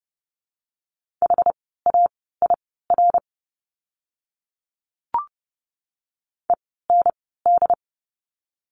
Morse code, heard as a keyed tone, about 30 wpm, spaced as for 17 wpm, 700 Hz. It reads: HAIR END
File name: Lesson 17f - Letter u - Words 30wpm